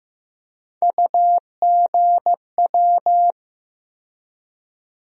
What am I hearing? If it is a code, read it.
Morse code, heard as a keyed tone, about 15 wpm, 700 Hz. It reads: UGW
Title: USA Call Sign Postfixes - 3 Letters 15wpm